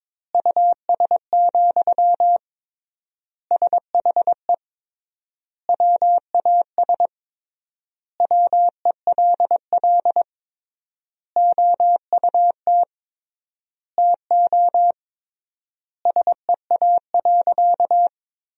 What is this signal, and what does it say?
Morse code, heard as a keyed tone, about 22 wpm, 700 Hz. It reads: US, SHE WAS WELL OUT TO SEA.
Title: 3x Word-Spacing - Sentences from Top 400 Words - 22wpm